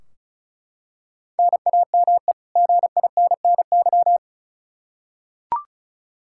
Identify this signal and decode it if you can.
Morse code, heard as a keyed tone, about 34 wpm, 700 Hz. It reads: NAME GINNY
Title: Practice QSO - Exchange 1 - Beginning - 35wpm